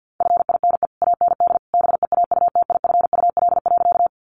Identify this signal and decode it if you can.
Morse code, heard as a keyed tone, about 50 wpm, 700 Hz. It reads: FINE AND BEAUTIFUL.
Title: Sentences from Top 900 Words 50wpm